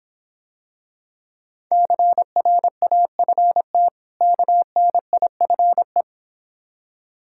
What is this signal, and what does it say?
Morse code, heard as a keyed tone, about 26 wpm, 690 Hz. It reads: CRAFT KNIFE